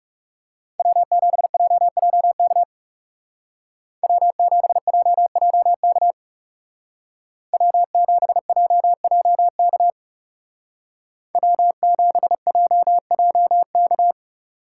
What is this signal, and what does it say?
Morse code, heard as a keyed tone, about 35 wpm, 700 Hz. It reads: W7JJK W7JJK W7JJK W7JJK
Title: Speed Racing - USA 1x3 Call Signs 30wpm